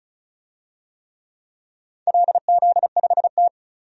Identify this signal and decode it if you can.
Morse code, heard as a keyed tone, about 36 wpm, 690 Hz. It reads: LZ5T
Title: DXCC-100 Call Signs - 4 Characters - 35wpm